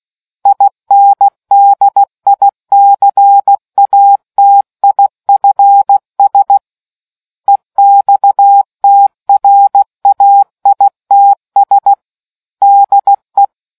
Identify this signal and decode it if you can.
Morse code, heard as a keyed tone, about 16 wpm, 800 Hz. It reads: INDICATIFS EXTRAITS DE